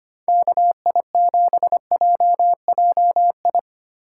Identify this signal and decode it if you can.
Morse code, heard as a keyed tone, about 25 wpm, 700 Hz. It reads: KI7JJI
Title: USA Call Signs - 6 Characters - 8x Character Spacing - 25wpm